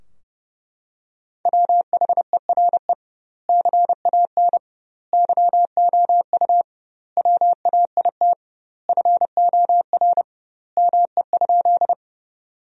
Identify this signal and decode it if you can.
Morse code, heard as a keyed tone, about 30 wpm, 700 Hz. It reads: WHERE CAN YOU WAIT FOR ME?
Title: 2x Word-Spacing - Sentences from Top 500 Words - 30wpm